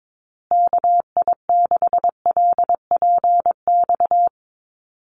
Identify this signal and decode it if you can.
Morse code, heard as a keyed tone, about 22 wpm, 700 Hz. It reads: KI6LPX
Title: USA Call Signs Repeated 3x 22wpm